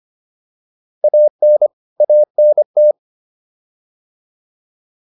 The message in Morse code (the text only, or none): AN ANT